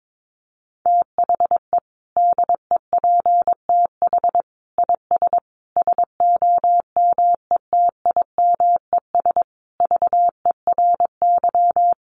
Morse code, heard as a keyed tone, about 22 wpm, 700 Hz. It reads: THE DEPTH IS SOMETIMES VERY
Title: Sentences from Top 1200 Words 22wpm